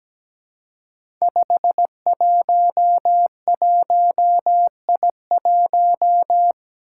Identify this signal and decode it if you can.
Morse code, heard as a keyed tone, about 17 wpm, 700 Hz. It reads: 511I1
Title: Five Challenging Letter-Number Combinations 17wpm